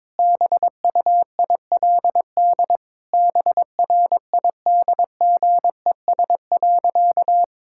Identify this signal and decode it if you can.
Morse code, heard as a keyed tone, about 22 wpm, 700 Hz. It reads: BUILD BRIDGES.
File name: Difficult Sentences with Hints - 22wpm